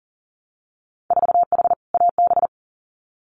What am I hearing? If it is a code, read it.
Morse code, heard as a keyed tone, about 40 wpm, 700 Hz. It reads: 4H AB
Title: Winter Field Day Exchanges Encore 40wpm